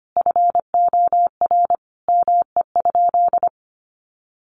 Morse code, heard as a keyed tone, about 25 wpm, 700 Hz. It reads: FOR ME?
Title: Sentences from Top 1100 Words 25wpm